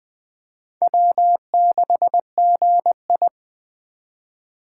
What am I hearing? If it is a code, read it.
Morse code, heard as a keyed tone, about 23 wpm, 700 Hz. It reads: W6GI
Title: Speed Racing - USA 1x2 Call Signs 20wpm